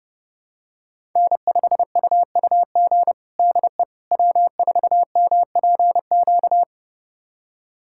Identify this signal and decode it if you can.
Morse code, heard as a keyed tone, about 32 wpm, 700 Hz. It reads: N5UUG DE W4MPQ
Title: USA Callsign DE Callsign - 5 Characters - 30wpm